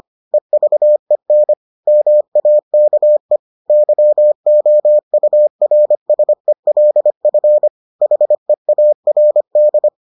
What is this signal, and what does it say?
Morse code, heard as a keyed tone, about 25 wpm, 600 Hz. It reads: EVEN MAKE YOURSELF HEARD